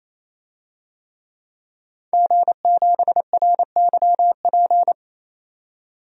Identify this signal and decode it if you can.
Morse code, heard as a keyed tone, about 28 wpm, 700 Hz. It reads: G7RYP
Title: DXCC-100 Call Signs - 5 Characters - 28wpm